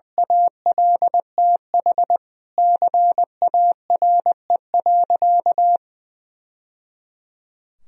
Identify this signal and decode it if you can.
Morse code, heard as a keyed tone, about 21 wpm, 700 Hz. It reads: ALTH CARE.
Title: Sentences from Top 1000 Words - Review Part 4 - 20wpm